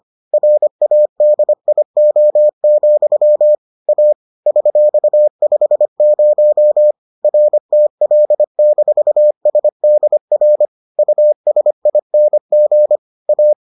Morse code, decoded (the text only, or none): RADIO, A $50 RTL-SDR USING A